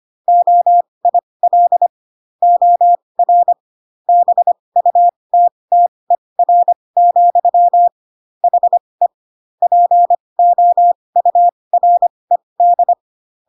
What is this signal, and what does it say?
Morse code, heard as a keyed tone, about 25 wpm, spaced as for 20 wpm, 700 Hz. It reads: OIL OR BUTTER, HE POURED